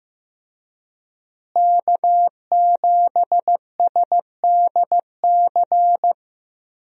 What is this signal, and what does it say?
Morse code, heard as a keyed tone, about 15 wpm, 700 Hz. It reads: K7SDC
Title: USA Call Signs - 5 Characters - Repeated 3x - 15wpm